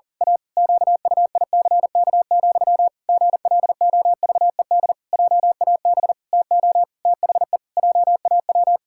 Morse code, heard as a keyed tone, about 40 wpm, 700 Hz. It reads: A QUICK, GLOVED JAB TO THE JAW
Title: Pangram Challenge v2 - 40wpm